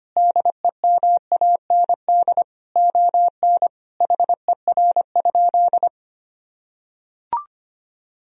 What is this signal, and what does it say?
Morse code, heard as a keyed tone, about 25 wpm, 700 Hz. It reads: DEMAND ON HER?
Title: Sentences from Top 1000 Words - Review Part 4 - 25wpm